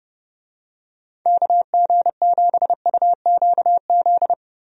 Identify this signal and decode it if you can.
Morse code, heard as a keyed tone, about 29 wpm, 700 Hz. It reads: KG7UQZ
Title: USA Call Signs - 6 Characters - Repeated 2x - 30wpm